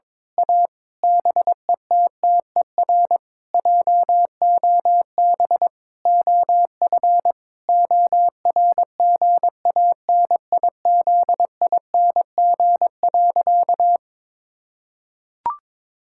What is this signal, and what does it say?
Morse code, heard as a keyed tone, about 22 wpm, 700 Hz. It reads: A BETTER JOB OF ORGANIZING.
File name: Sentences with 7-Word Phrases - 22wpm